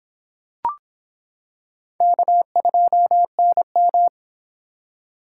Morse code, heard as a keyed tone, about 25 wpm, 700 Hz. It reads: K2NM